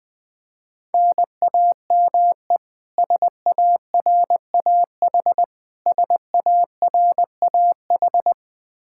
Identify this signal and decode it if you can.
Morse code, heard as a keyed tone, about 21 wpm, 700 Hz. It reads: NAME SARAH SARAH
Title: Practice QSO - Exchange 1 - Beginning - 20wpm